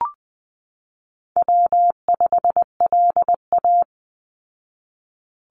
Morse code, heard as a keyed tone, about 19 wpm, 700 Hz. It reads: W5LA
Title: Call Signs Repeated 2x - 20wpm